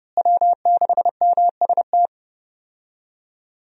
Morse code, heard as a keyed tone, about 31 wpm, 700 Hz. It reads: W6MST
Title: USA Call Signs - 5 Characters - Repeated 3x - 30wpm